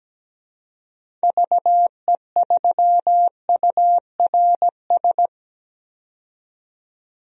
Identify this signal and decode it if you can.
Morse code, heard as a keyed tone, about 17 wpm, 700 Hz. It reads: VE3URS
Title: DXCC-100 Call Signs - 6 Characters - 17wpm